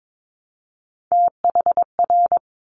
Morse code, heard as a keyed tone, about 22 wpm, 700 Hz. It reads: THR
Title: Top Three Letter Combinations in Words 22wpm